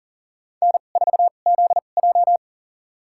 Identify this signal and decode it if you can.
Morse code, heard as a keyed tone, about 40 wpm, spaced as for 29 wpm, 700 Hz. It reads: N4ZJ
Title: USA Call Signs - 4 Characters - 2x Character Spacing - 40wpm